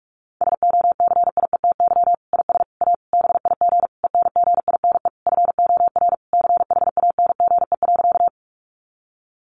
Morse code, heard as a keyed tone, about 44 wpm, 700 Hz. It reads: SOCIETY IS A BIG ENGINE FOR CHANGE.
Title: Sentences from Top 1500 Words 45wpm